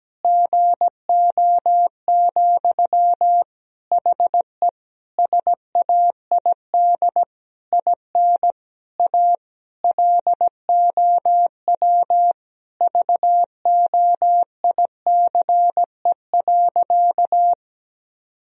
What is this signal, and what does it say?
Morse code, heard as a keyed tone, about 17 wpm, 700 Hz. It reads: GO, HE SAID IN A LOW VOICE.
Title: Sentences from Top 1000 Words - Review Part 1 - 17wpm